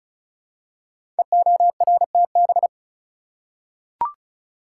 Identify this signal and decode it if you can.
Morse code, heard as a keyed tone, about 36 wpm, 700 Hz. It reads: EORTB
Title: Code Groups - 5 Letter:Numbers - 35wpm